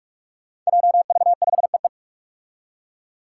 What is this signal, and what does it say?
Morse code, heard as a keyed tone, about 41 wpm, 700 Hz. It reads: JV5EE
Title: Five Challenging Letter-Number Combinations 45wpm